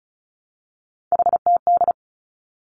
Morse code, heard as a keyed tone, about 34 wpm, 700 Hz. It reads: HTD
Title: Three Challenging Letter-Number Combinations 35wpm